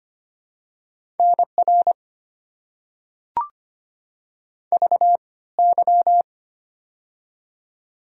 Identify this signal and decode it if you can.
Morse code, heard as a keyed tone, about 25 wpm, 690 Hz. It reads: NR V Y
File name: Top 90 QSO Elements - 3x Character Spacing - 25wpm